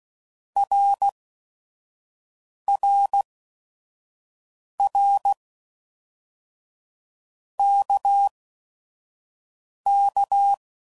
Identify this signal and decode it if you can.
Morse code, heard as a keyed tone, about 16 wpm, 800 Hz. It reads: R R R K K
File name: Basic Letter Groups By7Groups / RKPX / RKPX@5 f15